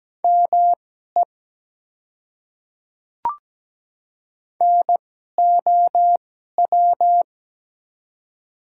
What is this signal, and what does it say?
Morse code, heard as a keyed tone, about 17 wpm, spaced as for 12 wpm, 700 Hz. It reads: ME NOW
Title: 2x Character Spacing - Top 200 Words - 17wpm